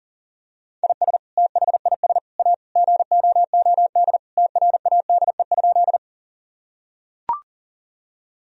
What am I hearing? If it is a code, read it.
Morse code, heard as a keyed tone, about 40 wpm, 700 Hz. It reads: IS THIS A GOOD TRADE?
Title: Sentences from Top 1000 Words - Review Part 4 - 40wpm